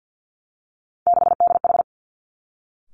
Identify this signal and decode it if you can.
Morse code, heard as a keyed tone, about 50 wpm, 700 Hz. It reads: 6DH
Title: Three Challenging Letter-Number Combinations 50wpm